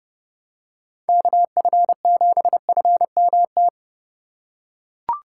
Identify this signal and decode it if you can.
Morse code, heard as a keyed tone, about 32 wpm, 700 Hz. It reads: KF7FMT
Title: USA Call Signs - 6 Characters - 4x Character Spacing - 30wpm